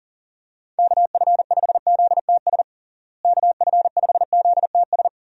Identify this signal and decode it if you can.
Morse code, heard as a keyed tone, about 40 wpm, 700 Hz. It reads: KF5ZTS KF5ZTS
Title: USA Call Signs - 6 Characters - Repeated 3x - 40wpm